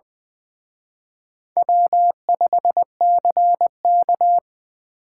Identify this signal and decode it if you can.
Morse code, heard as a keyed tone, about 23 wpm, 700 Hz. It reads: W5CK
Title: Speed Racing - USA 1x2 Call Signs 20wpm